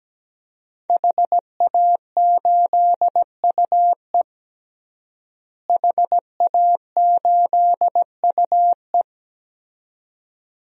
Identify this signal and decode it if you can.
Morse code, heard as a keyed tone, about 17 wpm, 700 Hz. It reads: HA8UE HA8UE
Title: DXCC-100 Call Signs - 5 Characters - Repeated 2x - 17wpm